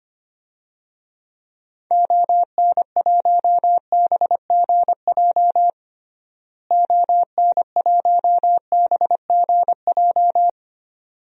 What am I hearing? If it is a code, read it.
Morse code, heard as a keyed tone, about 25 wpm, 700 Hz. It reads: ON1BGJ ON1BGJ
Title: DXCC-100 Call Signs - 6 Characters - Repeated 2x - 25wpm